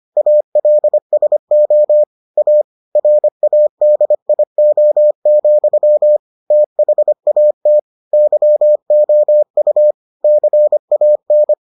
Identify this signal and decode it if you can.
Morse code, heard as a keyed tone, about 25 wpm, 600 Hz. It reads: ALSO A RADIO, THAT YOU CAN